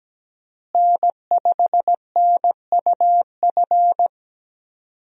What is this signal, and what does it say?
Morse code, heard as a keyed tone, about 17 wpm, 700 Hz. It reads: N5NUF
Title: USA Call Signs - 5 Characters - Repeated 3x - 17wpm